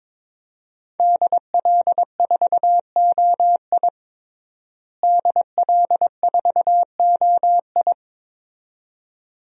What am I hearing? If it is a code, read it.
Morse code, heard as a keyed tone, about 22 wpm, 700 Hz. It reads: DL4OI DL4OI
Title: USA Call Signs - 5 Characters - Repeated 2x - 22wpm